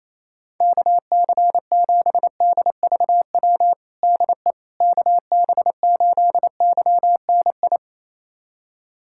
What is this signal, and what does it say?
Morse code, heard as a keyed tone, about 28 wpm, 700 Hz. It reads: KC7DVW DE KB8YNI
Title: USA Callsign DE Callsign 28wpm